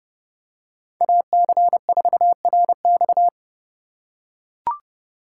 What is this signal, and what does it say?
Morse code, heard as a keyed tone, about 30 wpm, 700 Hz. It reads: AC4RX
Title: USA Call Signs - 5 Characters - 2x Character Spacing - 30wpm